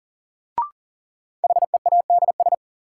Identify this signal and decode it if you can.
Morse code, heard as a keyed tone, about 40 wpm, 700 Hz. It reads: HEADS
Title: Sentences from Top 1000 Words - Review Part 3 - New Word Drill - 40wpm